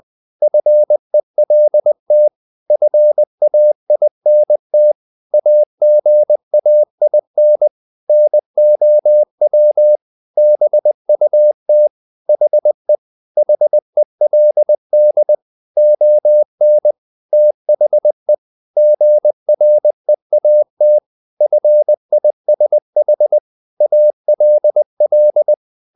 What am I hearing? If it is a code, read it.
Morse code, heard as a keyed tone, about 20 wpm, 600 Hz. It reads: FELT FAINT AGAIN NOW BUT HE HELD ON THE GREAT FISH ALL
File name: the old man and the sea part 2 20wpm